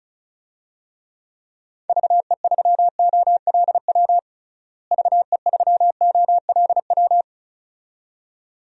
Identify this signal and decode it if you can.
Morse code, heard as a keyed tone, about 36 wpm, 700 Hz. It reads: VE3OLW VE3OLW
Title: DXCC-100 Call Signs - 6 Characters - Repeated 2x - 35wpm